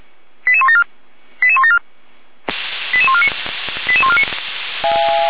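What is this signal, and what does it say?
Five-tone selective call (ZVEI), heard as quick runs of five tones, tones of about 70 ms. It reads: 88174 88174 88149 88149
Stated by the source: ZVEI five-tone call received on SDR.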